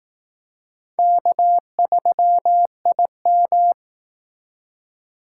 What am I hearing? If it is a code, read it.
Morse code, heard as a keyed tone, about 18 wpm, 700 Hz. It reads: K3IM
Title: Speed Racing - USA 1x2 Call Signs 15wpm